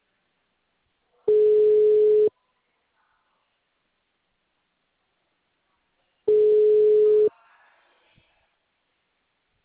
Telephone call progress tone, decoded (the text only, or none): ringback tone